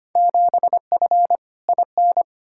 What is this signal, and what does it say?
Morse code, heard as a keyed tone, about 25 wpm, 700 Hz. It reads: 7F IN
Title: Field Day Exchanges Encore - 25wpm